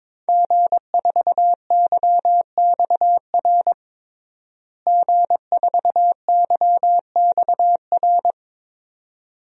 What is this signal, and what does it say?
Morse code, heard as a keyed tone, about 22 wpm, 700 Hz. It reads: G4YXR G4YXR
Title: DXCC-100 Call Signs - 5 Characters - Repeated 2x - 22wpm